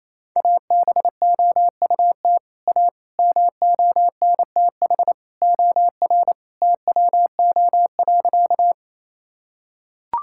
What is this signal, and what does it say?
Morse code, heard as a keyed tone, about 27 wpm, 700 Hz. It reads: ABOUT A MONTH OR TWO.
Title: Sentences from Top 1000 Words - Review Part 5 - 28wpm